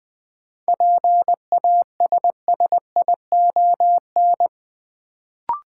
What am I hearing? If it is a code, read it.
Morse code, heard as a keyed tone, about 20 wpm, 700 Hz. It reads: PASSION
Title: Sentences from Top 1300 Words - New Word Drill - 20wpm